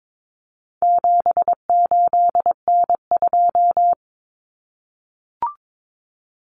Morse code, heard as a keyed tone, about 22 wpm, 700 Hz. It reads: 78N2
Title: Common Cut Numbers - 4-Digit 22wpm